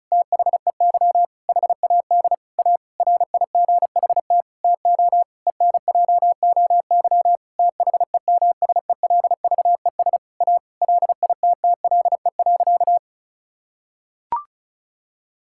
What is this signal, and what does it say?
Morse code, heard as a keyed tone, about 34 wpm, 700 Hz. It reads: THEY HAD A RIGHT TO ENJOY THEMSELVES A LITTLE.
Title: Sentences from Top 1000 Words - Review Part 5 - 35wpm